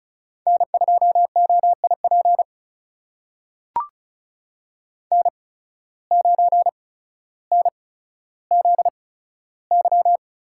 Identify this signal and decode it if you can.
Morse code, heard as a keyed tone, about 36 wpm, spaced as for 15 wpm, 700 Hz. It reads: N2OIP N9NZY
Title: USA Call Signs - 5 Characters - 8x Character Spacing - 35wpm